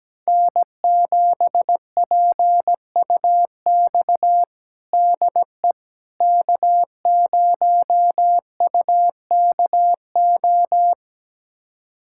Morse code, heard as a keyed tone, about 17 wpm, 700 Hz. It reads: N7PUX DE K0UKO
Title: USA Callsign DE Callsign - 5 Characters - 17wpm